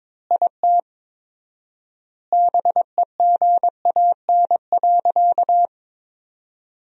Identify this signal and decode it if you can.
Morse code, heard as a keyed tone, about 22 wpm, 700 Hz. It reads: IT BEGAN.
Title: 4x Word-Spacing - Sentences from Top 400 Words - 22wpm